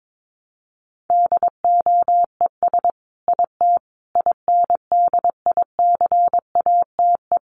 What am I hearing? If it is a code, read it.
Morse code, heard as a keyed tone, about 22 wpm, 700 Hz. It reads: DOES IT INDICATE
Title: Sentences from Top 1300 Words 22wpm